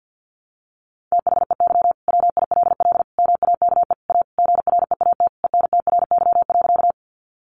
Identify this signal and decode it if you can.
Morse code, heard as a keyed tone, about 50 wpm, 700 Hz. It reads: THEY WILL MAKE A GREAT ENTRY.
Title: Sentences from Top 1500 Words 50wpm